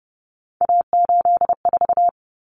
Morse code, heard as a keyed tone, about 30 wpm, 700 Hz. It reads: A84
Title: Common Cut Numbers - 3-Digit 30wpm